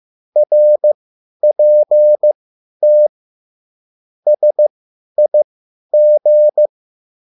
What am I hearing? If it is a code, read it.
Morse code, heard as a keyed tone, about 15 wpm, spaced as for 10 wpm, 600 Hz. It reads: RPT SIG
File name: QSL / CW lesson